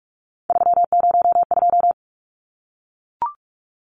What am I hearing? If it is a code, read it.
Morse code, heard as a keyed tone, about 44 wpm, 700 Hz. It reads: 302
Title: Numbers - 3-Digit 45wpm